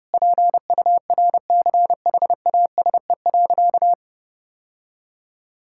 Morse code, heard as a keyed tone, about 30 wpm, 700 Hz. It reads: PURCHASE.
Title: Sentences from Top 1000 Words - Review Part 5 - 30wpm